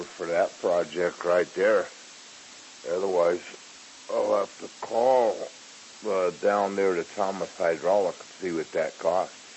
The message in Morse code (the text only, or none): none